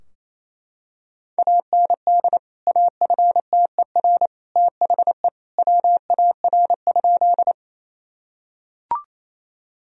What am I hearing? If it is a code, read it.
Morse code, heard as a keyed tone, about 28 wpm, 700 Hz. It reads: AND AFTER THE WAR?
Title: Sentences from Top 1000 Words - Review Part 2 - 28wpm